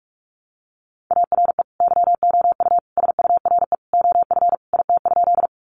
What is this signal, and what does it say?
Morse code, heard as a keyed tone, about 44 wpm, 700 Hz. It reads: ARE YOU SURE OF IT?
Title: Sentences from Top 300 Words 45wpm